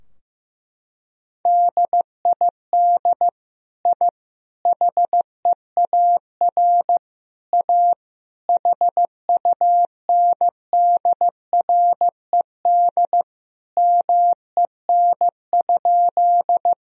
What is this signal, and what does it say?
Morse code, heard as a keyed tone, about 16 wpm, 700 Hz. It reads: DID I HEAR A HUNDRED MEN?
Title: Sentences from Top 1000 Words - Review Part 2 - 15wpm